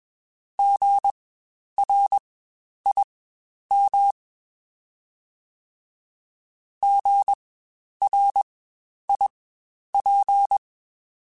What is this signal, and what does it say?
Morse code, heard as a keyed tone, about 21 wpm, spaced as for 10 wpm, 800 Hz. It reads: GRIM GRIP
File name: Spaced Alphas / Spaced Alphas mp3s / Gs sp @20